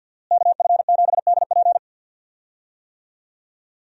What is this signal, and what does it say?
Morse code, heard as a keyed tone, about 47 wpm, 700 Hz. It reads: KF7DP
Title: USA Call Signs - 5 Characters - Repeated 2x - 50wpm